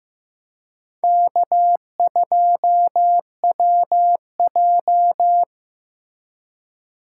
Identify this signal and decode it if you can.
Morse code, heard as a keyed tone, about 15 wpm, 700 Hz. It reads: K2WJ
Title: USA Call Signs - 8x Character Spacing - 15wpm